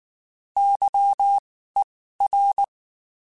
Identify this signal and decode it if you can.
Morse code, heard as a keyed tone, about 19 wpm, spaced as for 14 wpm, 800 Hz. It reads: YER